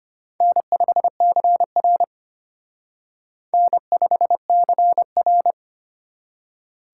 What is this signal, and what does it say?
Morse code, heard as a keyed tone, about 31 wpm, 700 Hz. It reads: N5CR N5CR